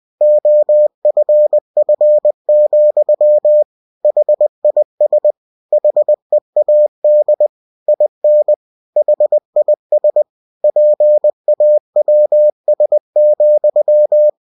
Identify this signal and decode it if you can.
Morse code, heard as a keyed tone, about 20 wpm, 600 Hz. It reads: OFF, HIS HEAD IN HIS PAWS,
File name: cw-books / winnie-the-pooh / winnie-20wpm-page-0057